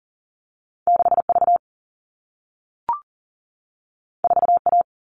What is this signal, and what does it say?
Morse code, heard as a keyed tone, about 42 wpm, 700 Hz. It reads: 6V 4A
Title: All Cut Numbers - 2-Digit 40wpm